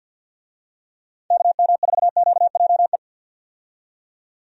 Morse code, heard as a keyed tone, about 51 wpm, 700 Hz. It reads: KM4QJE
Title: USA Call Signs - 6 Characters - 50wpm